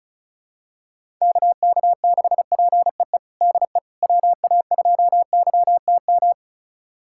Morse code, heard as a keyed tone, about 36 wpm, 700 Hz. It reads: KK6PEE DE WA2YTM
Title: USA Callsign DE Callsign - 6 Characters - 35wpm